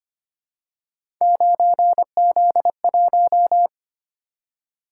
Morse code, heard as a keyed tone, about 25 wpm, 700 Hz. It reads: 9Z1
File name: Three Letter-Number Combinations - 25wpm